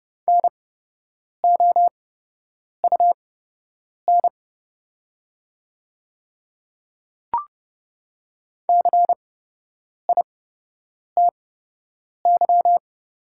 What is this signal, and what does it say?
Morse code, heard as a keyed tone, about 30 wpm, spaced as for 8 wpm, 690 Hz. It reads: NOUN CITY
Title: 8x Character Spacing - Top 500 Words - 30wpm